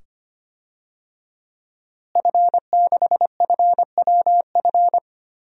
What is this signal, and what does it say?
Morse code, heard as a keyed tone, about 25 wpm, 700 Hz. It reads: F6FWF
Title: DXCC-100 Call Signs - 5 Characters - Repeated 2x - 25wpm